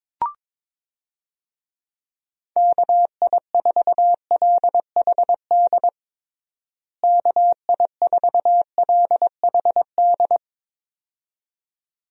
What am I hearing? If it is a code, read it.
Morse code, heard as a keyed tone, about 22 wpm, 700 Hz. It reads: KI4LHD KI4LHD
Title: USA Call Signs - 6 Characters - Repeated 2x - 22wpm